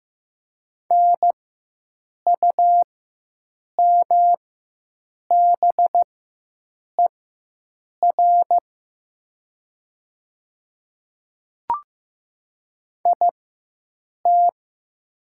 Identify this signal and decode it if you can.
Morse code, heard as a keyed tone, about 15 wpm, spaced as for 7 wpm, 700 Hz. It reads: NUMBER IT
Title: 4x Character Spacing - Top 100 Words - 15wpm